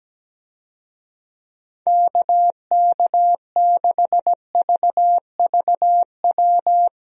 Morse code, heard as a keyed tone, about 17 wpm, 700 Hz. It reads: KK6VVW
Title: USA Call Signs - 6 Characters - 17wpm